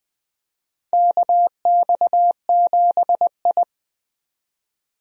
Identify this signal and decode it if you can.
Morse code, heard as a keyed tone, about 21 wpm, 700 Hz. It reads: KX7I
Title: USA Call Signs - 4 Characters - 20wpm